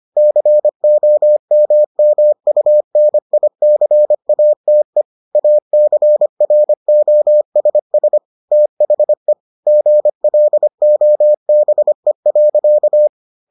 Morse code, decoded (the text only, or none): COMMUNICATE ACROSS THE GLOBE.